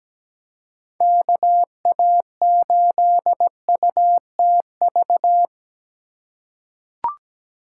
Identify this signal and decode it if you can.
Morse code, heard as a keyed tone, about 17 wpm, 700 Hz. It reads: KA8UTV
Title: USA Call Signs - 6 Characters - 2x Character Spacing - 17wpm